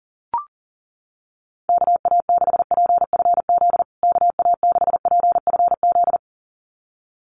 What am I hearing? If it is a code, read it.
Morse code, heard as a keyed tone, about 37 wpm, 700 Hz. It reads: KA6PFZ KA6PFZ
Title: USA Call Signs Repeated 2x 40wpm